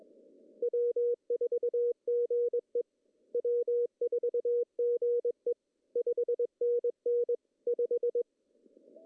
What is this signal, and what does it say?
Morse code, heard as a keyed tone, about 21 wpm, 440 Hz. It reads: W4GE W4GE 5NN 5
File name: CW / cqww-cw-k5tr